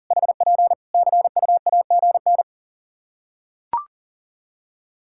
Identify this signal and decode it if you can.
Morse code, heard as a keyed tone, about 40 wpm, 700 Hz. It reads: HP CUAGN
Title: Practice QSO - Exchange 4 - Ending - 40wpm